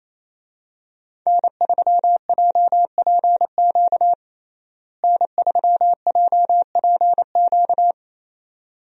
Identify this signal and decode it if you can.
Morse code, heard as a keyed tone, about 27 wpm, 700 Hz. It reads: N3JPQ N3JPQ
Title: USA Call Signs - 5 Characters - Repeated 2x - 28wpm